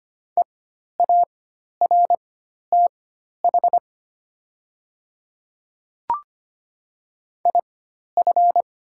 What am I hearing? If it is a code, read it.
Morse code, heard as a keyed tone, about 25 wpm, spaced as for 12 wpm, 700 Hz. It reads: EARTH IF
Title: 4x Character Spacing - Top 200 Words - 25wpm